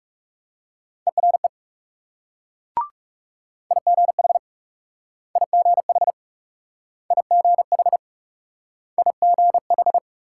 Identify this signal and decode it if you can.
Morse code, heard as a keyed tone, about 42 wpm, 700 Hz. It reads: ERE IGH IGH IGH IGH